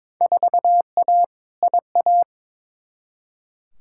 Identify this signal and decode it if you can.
Morse code, heard as a keyed tone, about 22 wpm, 700 Hz. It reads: 4A IA
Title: Field Day Exchanges - 22wpm